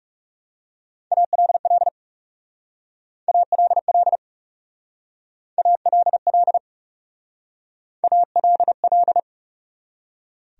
Morse code, heard as a keyed tone, about 38 wpm, 700 Hz. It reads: ALL ALL ALL ALL